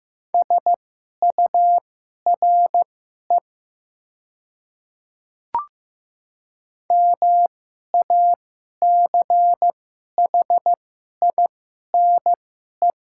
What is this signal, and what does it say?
Morse code, heard as a keyed tone, about 15 wpm, spaced as for 11 wpm, 700 Hz. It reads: SURE MACHINE
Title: 2x Character Spacing - Top 500 Words - 15wpm